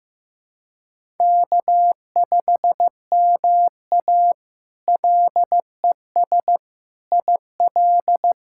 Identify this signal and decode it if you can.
Morse code, heard as a keyed tone, about 15 wpm, 700 Hz. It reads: K5MA LES IL
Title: K1USN SST QSO Practice - 15wpm